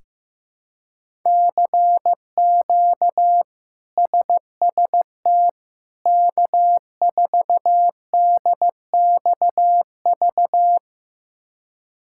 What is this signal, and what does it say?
Morse code, heard as a keyed tone, about 15 wpm, 700 Hz. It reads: CQ SST K4DXV
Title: K1USN SST QSO Practice - 15wpm